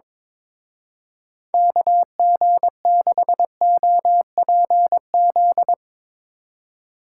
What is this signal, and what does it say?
Morse code, heard as a keyed tone, about 22 wpm, 700 Hz. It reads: KG6OPZ
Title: Speed Racing - USA 2x3 Call Signs 22wpm